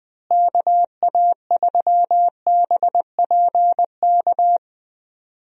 Morse code, heard as a keyed tone, about 20 wpm, 700 Hz. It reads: KA3BPK
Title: USA Call Signs - 6 Characters - Repeated 3x - 20wpm